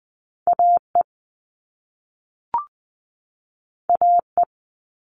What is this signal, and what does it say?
Morse code, heard as a keyed tone, about 20 wpm, 700 Hz. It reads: AE AE
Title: All Cut Numbers - 2-Digit 20wpm